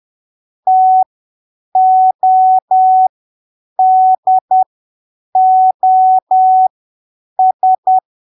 Morse code, heard as a keyed tone, about 10 wpm, spaced as for 7 wpm, 750 Hz. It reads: TODOS